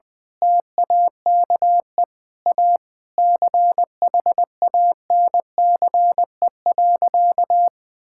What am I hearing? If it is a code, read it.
Morse code, heard as a keyed tone, about 20 wpm, 700 Hz. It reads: TAKE A CHANCE.
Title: Sentences from Top 1000 Words - Review Part 6 - 20wpm